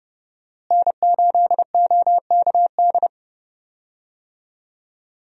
Morse code, heard as a keyed tone, about 30 wpm, 700 Hz. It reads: N8OKD
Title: USA Call Signs - 5 Characters - Repeated 2x - 30wpm